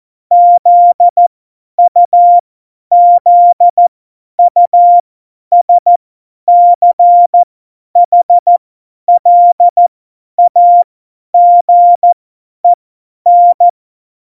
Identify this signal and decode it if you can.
Morse code, heard as a keyed tone, about 14 wpm, spaced as for 10 wpm, 700 Hz. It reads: ZUZUSCHLAGEN
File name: Morsekurs / Witze / Witz 31. Oktober 17, Tempo 50